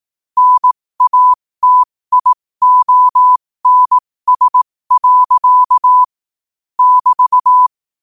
Morse code, heard as a keyed tone, about 18 wpm, spaced as for 15 wpm, 1000 Hz. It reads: NATIONS. <BT>
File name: qotdfiles / 15wpm